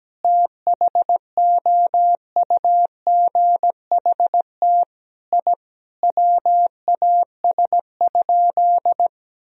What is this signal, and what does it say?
Morse code, heard as a keyed tone, about 17 wpm, 700 Hz. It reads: THOUGHT I WAS?
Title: Sentences from Top 1000 Words - Review Part 3 - 17wpm